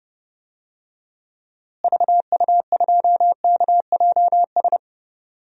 Morse code, heard as a keyed tone, about 30 wpm, 700 Hz. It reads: VU2KJS
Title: DXCC-100 Call Signs - 6 Characters - Repeated 3x - 30wpm